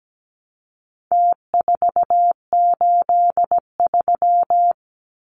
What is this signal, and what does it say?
Morse code, heard as a keyed tone, about 17 wpm, 700 Hz. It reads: T483
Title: Numbers - Contest Style - 4-Digit 17wpm